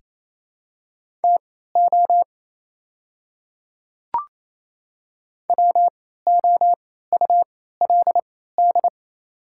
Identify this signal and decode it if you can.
Morse code, heard as a keyed tone, about 28 wpm, spaced as for 16 wpm, 700 Hz. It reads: TO WOULD
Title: 3x Character Spacing - Top 100 Words - 28wpm